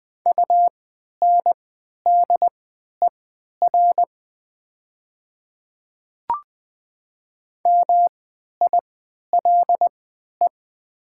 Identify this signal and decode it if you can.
Morse code, heard as a keyed tone, about 20 wpm, spaced as for 11 wpm, 700 Hz. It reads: UNDER MILE
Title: 3x Character Spacing - Top 300 Words - 20wpm